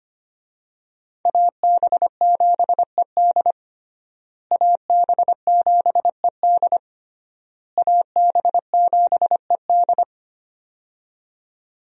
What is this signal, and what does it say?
Morse code, heard as a keyed tone, about 25 wpm, 700 Hz. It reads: AB7ED AB7ED AB7ED
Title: USA Call Signs - 5 Characters - Repeated 3x - 25wpm